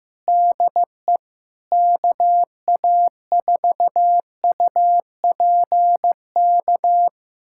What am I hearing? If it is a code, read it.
Morse code, heard as a keyed tone, about 15 wpm, 700 Hz. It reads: DE KA4UPK
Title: USA Callsign DE Callsign 15wpm